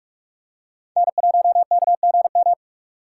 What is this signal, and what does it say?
Morse code, heard as a keyed tone, about 44 wpm, 700 Hz. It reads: N1KGM